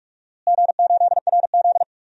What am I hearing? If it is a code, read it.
Morse code, heard as a keyed tone, about 46 wpm, 700 Hz. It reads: G8RZ
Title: DXCC-100 Call Signs - 4 Characters - Repeated 3x - 45wpm